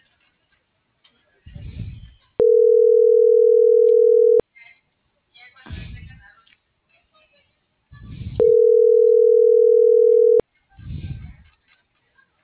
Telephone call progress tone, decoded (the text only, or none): ringback tone